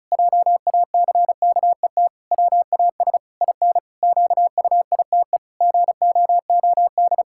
Morse code, heard as a keyed tone, about 36 wpm, 700 Hz. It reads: JACKET WAS IN QUITE GOOD
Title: Pangram Challenge v2 - 35wpm